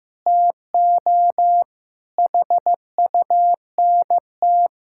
Morse code, heard as a keyed tone, about 15 wpm, 700 Hz. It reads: TO HUNT
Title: Sentences from Top 1600 Words 15wpm